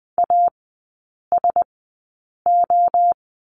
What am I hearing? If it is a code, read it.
Morse code, heard as a keyed tone, about 20 wpm, spaced as for 8 wpm, 700 Hz. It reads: ASO
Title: 035 Lesson 7e - Letter s - Three Characters 20wpm